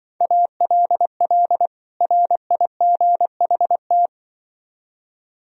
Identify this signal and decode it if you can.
Morse code, heard as a keyed tone, about 23 wpm, 700 Hz. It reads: ALL RIGHT
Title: Speed-Racing - Sets of 2 Words 20wpm